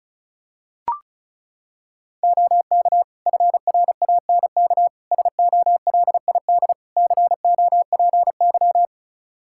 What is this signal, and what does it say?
Morse code, heard as a keyed tone, about 34 wpm, 700 Hz. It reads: OK FRANK SOLID COPY
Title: Practice QSO - Exchange 3 - More Info - 35wpm